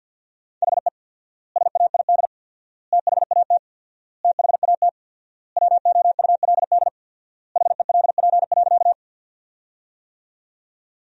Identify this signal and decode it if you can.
Morse code, heard as a keyed tone, about 48 wpm, 700 Hz. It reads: HE SAID THAT THAT WOULD HELP.
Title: 4x Word-Spacing - Sentences from Top 200 Words - 50wpm